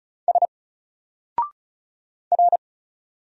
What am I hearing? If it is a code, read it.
Morse code, heard as a keyed tone, about 36 wpm, 700 Hz. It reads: S R
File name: Single-Letter-Number-35wpm